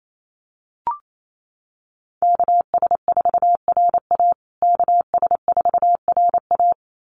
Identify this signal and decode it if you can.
Morse code, heard as a keyed tone, about 28 wpm, 700 Hz. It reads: KS4RA KS4RA
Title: USA Call Signs Repeated 2x 28wpm